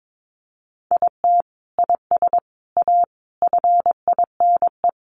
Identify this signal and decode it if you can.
Morse code, heard as a keyed tone, about 22 wpm, 700 Hz. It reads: IT IS A FINE
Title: Sentences from Top 1000 Words 22wpm